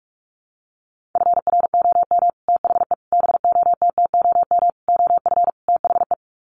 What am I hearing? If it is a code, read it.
Morse code, heard as a keyed tone, about 44 wpm, 700 Hz. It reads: FROM THE BOTTOM OF THE
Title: Sets of 5 Words Encore2 45wpm